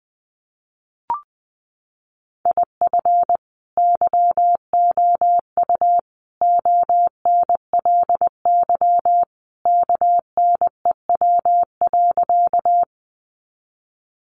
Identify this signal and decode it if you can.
Morse code, heard as a keyed tone, about 20 wpm, 700 Hz. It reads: IF YOU ONLY KNEW.
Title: Sentences from Top 400 Words 20wpm